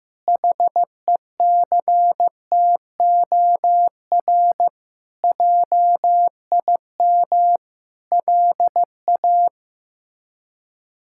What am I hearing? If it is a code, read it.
Morse code, heard as a keyed tone, about 15 wpm, 700 Hz. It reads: HECTOR JIM LA
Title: K1USN SST QSO Practice - 15wpm